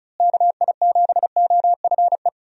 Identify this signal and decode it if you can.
Morse code, heard as a keyed tone, about 36 wpm, 700 Hz. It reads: KI7OFE
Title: USA Call Signs - 6 Characters - Repeated 3x - 35wpm